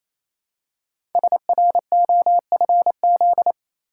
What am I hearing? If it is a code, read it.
Morse code, heard as a keyed tone, about 28 wpm, 700 Hz. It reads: SROFZ
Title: Code Groups - 5 Letters - 28wpm